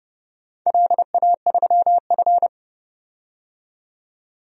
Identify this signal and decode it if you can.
Morse code, heard as a keyed tone, about 30 wpm, 700 Hz. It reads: LA3F
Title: DXCC-100 Call Signs - 4 Characters - 30wpm